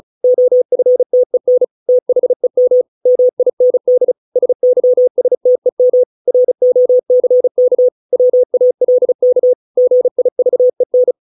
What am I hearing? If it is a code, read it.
Morse code, heard as a keyed tone, about 34 wpm, 500 Hz. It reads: OFTEN THEM MIND SYSTEM ROCK WALK GIVEN